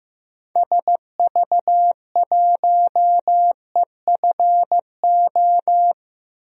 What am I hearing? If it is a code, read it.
Morse code, heard as a keyed tone, about 15 wpm, 700 Hz. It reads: SV1EFO
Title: DXCC-100 Call Signs - 6 Characters - Repeated 3x - 15wpm